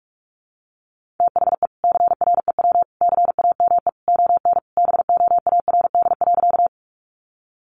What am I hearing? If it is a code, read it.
Morse code, heard as a keyed tone, about 44 wpm, 700 Hz. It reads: THE CREW CAME ON BOARD.
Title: Sentences from Top 1500 Words 45wpm